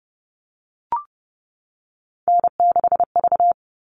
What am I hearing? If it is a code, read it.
Morse code, heard as a keyed tone, about 30 wpm, 700 Hz. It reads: N6V
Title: All Cut Numbers - 3-Digit 30wpm